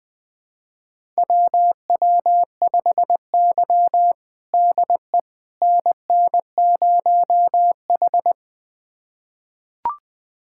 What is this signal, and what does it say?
Morse code, heard as a keyed tone, about 21 wpm, 700 Hz. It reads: WW5Y DE NN0H
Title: USA Callsign DE Callsign - 4 Characters - 20wpm